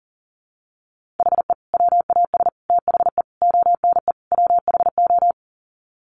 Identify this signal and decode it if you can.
Morse code, heard as a keyed tone, about 40 wpm, 700 Hz. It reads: HE WAS THE ONE WHO
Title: Sets of 5 Words Encore2 40wpm